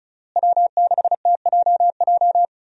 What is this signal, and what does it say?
Morse code, heard as a keyed tone, about 34 wpm, 700 Hz. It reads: W6TJJ
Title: USA Call Signs - 5 Characters - Repeated 3x - 35wpm